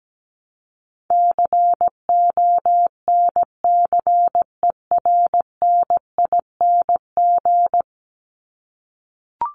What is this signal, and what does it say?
Morse code, heard as a keyed tone, about 17 wpm, 700 Hz. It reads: CONCERNING
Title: ING Suffix - Top 100 Words with Base Words 17wpm